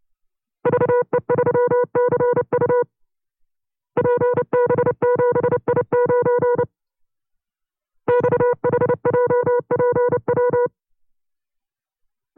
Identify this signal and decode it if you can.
Morse code, heard as a keyed tone, about 29 wpm, 490 Hz. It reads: VE3CU PB7I9 XHJPW
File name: CW-practice/25WPM